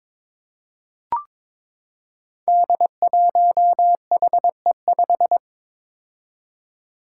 Morse code, heard as a keyed tone, about 22 wpm, 700 Hz. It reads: D1HE5
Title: Five Challenging Letter-Number Combinations 22wpm